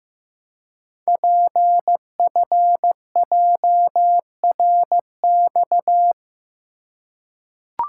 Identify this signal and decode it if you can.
Morse code, heard as a keyed tone, about 15 wpm, 700 Hz. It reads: PFJRX
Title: Code Groups - 5 Letters - 15wpm